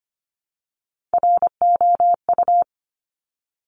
Morse code, heard as a keyed tone, about 25 wpm, 700 Hz. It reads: ROU
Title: Top Three Letter Combinations in Words 25wpm